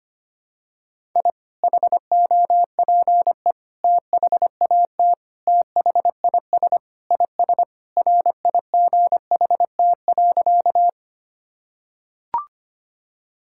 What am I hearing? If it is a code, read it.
Morse code, heard as a keyed tone, about 25 wpm, 700 Hz. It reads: I HOPE THAT THIS IS RIGHT.
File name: practice / 25wpm / Sentences with 3-Word Phrases - 25wpm